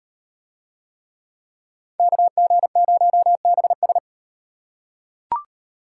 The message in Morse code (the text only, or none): KG0BS